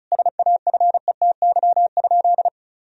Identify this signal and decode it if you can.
Morse code, heard as a keyed tone, about 34 wpm, 700 Hz. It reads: SAFETY?
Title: Sentences from Top 1000 Words - Review Part 4 - 35wpm